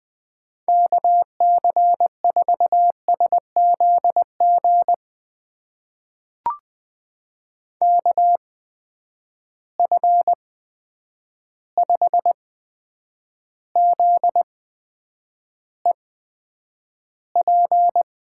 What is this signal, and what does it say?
Morse code, heard as a keyed tone, about 20 wpm, spaced as for 9 wpm, 700 Hz. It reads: KC4SZG KF5ZEP
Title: USA Call Signs - 6 Characters - 8x Character Spacing - 20wpm